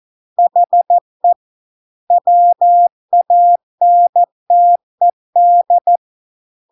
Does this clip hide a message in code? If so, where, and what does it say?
Yes: Morse code from 0.4 to 6.0 s, heard as a keyed tone, about 14 wpm, 700 Hz. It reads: HE WANTED